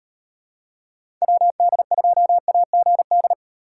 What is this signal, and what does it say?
Morse code, heard as a keyed tone, about 37 wpm, 700 Hz. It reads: WD2AGD